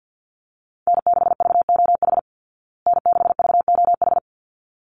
Morse code, heard as a keyed tone, about 49 wpm, 700 Hz. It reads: N6VOH N6VOH
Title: Call Signs Repeated 3x - 50wpm